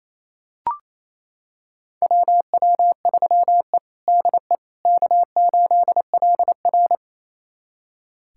Morse code, heard as a keyed tone, about 28 wpm, 700 Hz. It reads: WW3E DE K8LR
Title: USA Callsign DE Callsign - 4 Characters - 28wpm